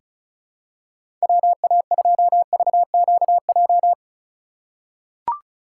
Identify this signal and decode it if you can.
Morse code, heard as a keyed tone, about 36 wpm, 700 Hz. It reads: WA2VQJ
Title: USA Call Signs - 4x Character Spacing - 35wpm